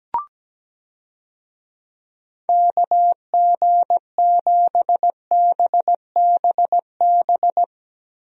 Morse code, heard as a keyed tone, about 17 wpm, 700 Hz. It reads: KG7BBB
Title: USA Call Signs - 6 Characters - Repeated 2x - 17wpm